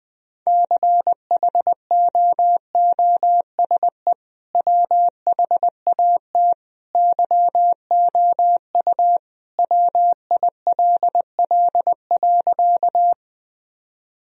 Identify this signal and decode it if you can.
Morse code, heard as a keyed tone, about 20 wpm, 700 Hz. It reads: CHOOSE WHAT YOU WILL.
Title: Sentences from Top 1000 Words - Review Part 4 - 20wpm